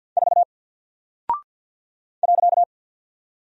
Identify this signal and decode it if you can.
Morse code, heard as a keyed tone, about 50 wpm, 700 Hz. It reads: 4 .
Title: Single Character (Letter:Number:Punctuation:Prosign) Alt 50wpm